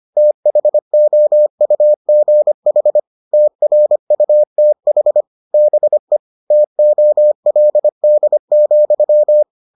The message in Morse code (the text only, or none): THOUGH TRUTH BE TOLD,